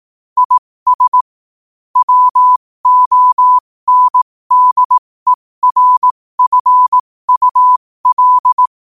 Morse code, heard as a keyed tone, about 18 wpm, spaced as for 15 wpm, 1000 Hz. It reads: IS WONDERFUL
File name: qotdfiles / 15wpm